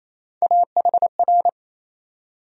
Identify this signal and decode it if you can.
Morse code, heard as a keyed tone, about 28 wpm, 700 Hz. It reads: AHR
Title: USA Call Sign Postfixes - 3 Letters 28wpm